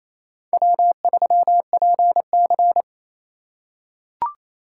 Morse code, heard as a keyed tone, about 27 wpm, 700 Hz. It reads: W3PC
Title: USA Call Signs - 4 Characters - 4x Character Spacing - 28wpm